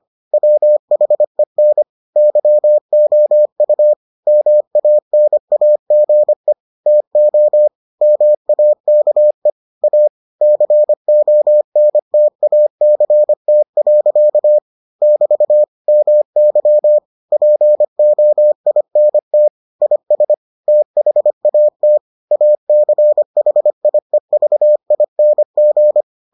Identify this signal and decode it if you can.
Morse code, heard as a keyed tone, about 25 wpm, 600 Hz. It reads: WHEN YOU MANAGE TO MAKE A CONTACT. <BT> MY POINT IS THAT ACHIEVING